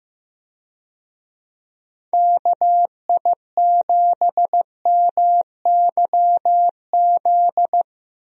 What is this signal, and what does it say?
Morse code, heard as a keyed tone, about 15 wpm, 700 Hz. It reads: KI7MYZ
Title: USA Call Signs - 6 Characters - 15wpm